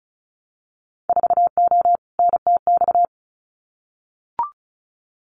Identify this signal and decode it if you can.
Morse code, heard as a keyed tone, about 34 wpm, 700 Hz. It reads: 4O NTX
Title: Winter Field Day Exchanges 35wpm